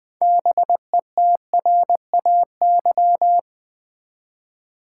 Morse code, heard as a keyed tone, about 20 wpm, 700 Hz. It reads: BETRAY
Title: Build A Word - 2 Letters At A Time - 6-Letter Words - 20wpm